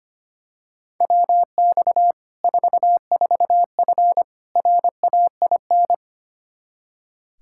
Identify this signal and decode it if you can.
Morse code, heard as a keyed tone, about 25 wpm, 700 Hz. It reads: WX 44F RAIN
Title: Practice QSO - Exchange 2 - Info - 25wpm